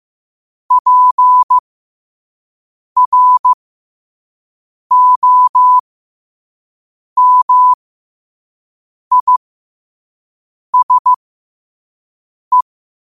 Morse code, heard as a keyed tone, about 15 wpm, spaced as for 5 wpm, 1000 Hz. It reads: PROMISE
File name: qotdfiles / 5wpm